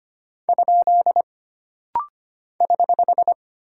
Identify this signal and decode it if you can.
Morse code, heard as a keyed tone, about 25 wpm, 700 Hz. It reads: ? <HH>
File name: Single Punctuation-Prosign - Rapid-Fire - 25wpm